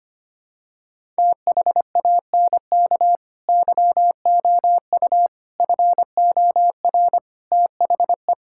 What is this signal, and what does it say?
Morse code, heard as a keyed tone, about 25 wpm, 700 Hz. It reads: THANK YOU FOR THE
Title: Sentences from Top 1000 Words - Review Part 6 - 25wpm